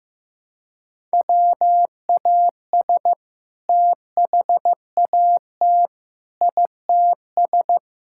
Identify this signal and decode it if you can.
Morse code, heard as a keyed tone, about 15 wpm, 700 Hz. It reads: WAS THAT ITS
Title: Sentences from Top 1000 Words - Review Part 5 - 15wpm